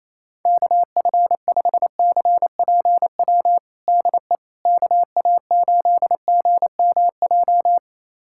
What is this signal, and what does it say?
Morse code, heard as a keyed tone, about 28 wpm, 700 Hz. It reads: KF5CPW DE KA8GMJ
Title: USA Callsign DE Callsign - 6 Characters - 28wpm